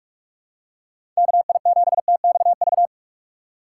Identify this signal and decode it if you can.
Morse code, heard as a keyed tone, about 48 wpm, 700 Hz. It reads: KI7TXV
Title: USA Call Signs - 4x Character Spacing - 45wpm